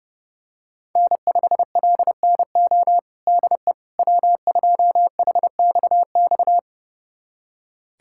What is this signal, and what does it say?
Morse code, heard as a keyed tone, about 32 wpm, 700 Hz. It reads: N5LNO DE W2HXX
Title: USA Callsign DE Callsign - 5 Characters - 30wpm